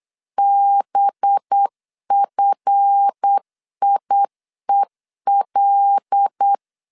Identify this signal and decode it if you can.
Morse code, heard as a keyed tone, about 8 wpm, 800 Hz. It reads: BFIEL
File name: LEZIONI / 7 GF9Z / 7 GF9 8wpm 3545